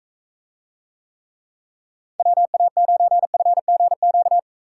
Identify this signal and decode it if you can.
Morse code, heard as a keyed tone, about 44 wpm, 700 Hz. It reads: WA9FGQ